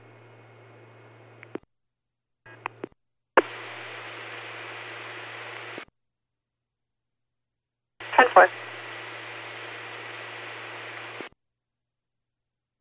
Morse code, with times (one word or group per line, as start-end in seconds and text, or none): none